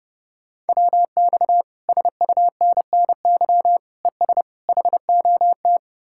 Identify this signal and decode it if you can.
Morse code, heard as a keyed tone, about 38 wpm, 700 Hz. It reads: WX SUNNY ES HOT